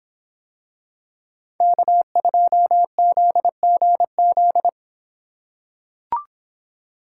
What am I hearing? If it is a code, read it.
Morse code, heard as a keyed tone, about 26 wpm, 700 Hz. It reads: K2ZGZ